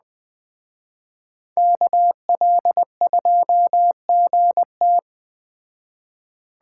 Morse code, heard as a keyed tone, about 20 wpm, 700 Hz. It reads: KL2GT
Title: Speed Racing - USA Call Signs 17wpm